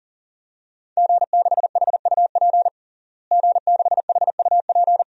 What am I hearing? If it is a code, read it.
Morse code, heard as a keyed tone, about 40 wpm, 700 Hz. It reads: G6HUP G6HUP
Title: DXCC-100 Call Signs - 5 Characters - Repeated 2x - 40wpm